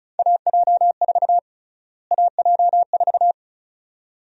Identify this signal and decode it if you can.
Morse code, heard as a keyed tone, about 36 wpm, 700 Hz. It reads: AJ4 AJ4
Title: USA Call Sign Prefixes - Repeated 2x - 35wpm